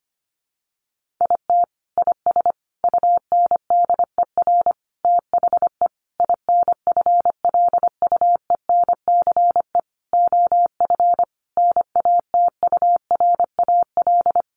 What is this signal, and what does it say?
Morse code, heard as a keyed tone, about 25 wpm, 700 Hz. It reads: IT IS UNDER THE INFLUENCE OF NATURAL
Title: Sentences from Top 1400 Words 25wpm